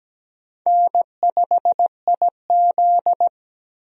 Morse code, heard as a keyed tone, about 17 wpm, 700 Hz. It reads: N5IZ
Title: USA Call Signs - 4 Characters - Repeated 2x - 17wpm